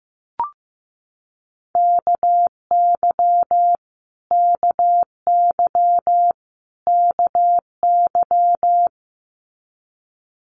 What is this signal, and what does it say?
Morse code, heard as a keyed tone, about 15 wpm, 700 Hz. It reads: KY KY KY
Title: US States Repeated 3x 15wpm